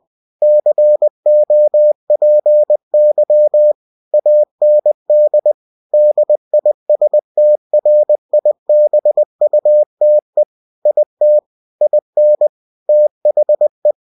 Morse code, heard as a keyed tone, about 20 wpm, 600 Hz. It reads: COPY AND DISTRIBUTE IT IN THE
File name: cw-books / winnie-the-pooh / winnie-20wpm-page-0112